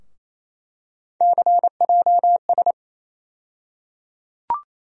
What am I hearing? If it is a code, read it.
Morse code, heard as a keyed tone, about 28 wpm, 700 Hz. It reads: CJS
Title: USA Call Sign Postfixes - 3 Letters 28wpm